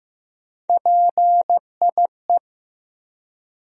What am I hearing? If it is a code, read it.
Morse code, heard as a keyed tone, about 15 wpm, 700 Hz. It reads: PIE
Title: USA Call Sign Postfixes - 3 Letters 15wpm